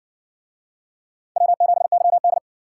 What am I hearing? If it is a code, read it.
Morse code, heard as a keyed tone, about 58 wpm, 700 Hz. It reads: W7YD